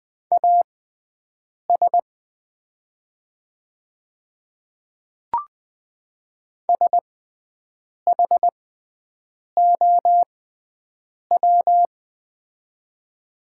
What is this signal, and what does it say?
Morse code, heard as a keyed tone, about 20 wpm, spaced as for 7 wpm, 700 Hz. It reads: AS SHOW
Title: 6x Character Spacing - Top 300 Words - 20wpm